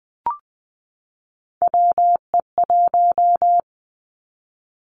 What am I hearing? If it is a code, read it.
Morse code, heard as a keyed tone, about 20 wpm, 700 Hz. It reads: WE1
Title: Three Challenging Letter-Number Combinations 20wpm